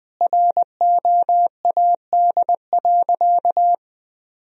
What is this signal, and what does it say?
Morse code, heard as a keyed tone, about 20 wpm, 700 Hz. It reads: ROAD.
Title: Sentences from Top 1000 Words - Review Part 1 - 20wpm